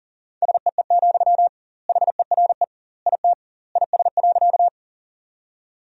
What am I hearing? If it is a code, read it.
Morse code, heard as a keyed tone, about 40 wpm, 700 Hz. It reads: SEE, HERE IT IS.
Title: 2x Word-Spacing - Sentences from Top 200 Words - 40wpm